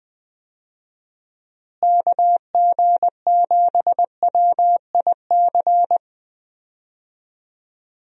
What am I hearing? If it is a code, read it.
Morse code, heard as a keyed tone, about 20 wpm, 700 Hz. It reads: KG7WIC
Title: USA Call Signs - 6 Characters - Repeated 2x - 20wpm